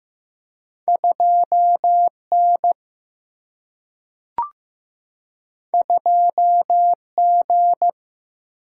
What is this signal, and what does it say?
Morse code, heard as a keyed tone, about 15 wpm, 700 Hz. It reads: 2 N 2 G
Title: Single-Letter Single-Number 15wpm